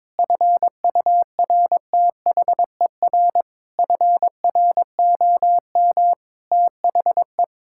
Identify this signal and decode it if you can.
Morse code, heard as a keyed tone, about 22 wpm, 700 Hz. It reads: FURTHER FROM THE
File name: Sentences with 7-Word Phrases - 22wpm